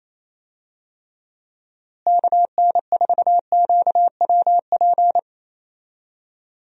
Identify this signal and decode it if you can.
Morse code, heard as a keyed tone, about 28 wpm, 700 Hz. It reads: KN4QWP
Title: USA Call Signs - 6 Characters - Repeated 3x - 28wpm